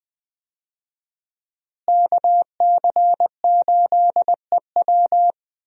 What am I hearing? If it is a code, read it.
Morse code, heard as a keyed tone, about 20 wpm, 700 Hz. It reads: KC8EW
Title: USA Call Signs - 5 Characters - Repeated 3x - 20wpm